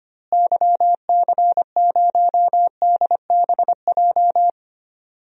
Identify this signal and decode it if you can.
Morse code, heard as a keyed tone, about 25 wpm, 700 Hz. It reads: YC0DBJ
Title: DXCC-100 Call Signs - 6 Characters - Repeated 2x - 25wpm